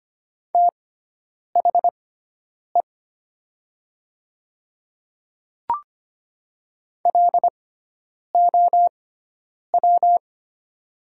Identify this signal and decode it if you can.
Morse code, heard as a keyed tone, about 25 wpm, spaced as for 9 wpm, 700 Hz. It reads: THE LOW
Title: 6x Character Spacing - Top 500 Words - 25wpm